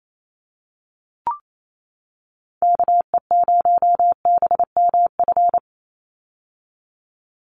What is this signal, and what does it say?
Morse code, heard as a keyed tone, about 28 wpm, 700 Hz. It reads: KE0BMF
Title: USA Call Signs 28wpm